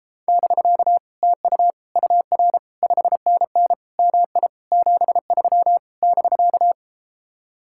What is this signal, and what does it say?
Morse code, heard as a keyed tone, about 33 wpm, 700 Hz. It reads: <BK> TU UR 5NN MI 73 <BK>